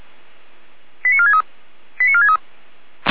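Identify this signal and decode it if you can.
Five-tone selective call (ZVEI), heard as quick runs of five tones, tones of about 70 ms. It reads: 89463 89463
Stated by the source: Five-tone ZVEI selective call received on an SDR.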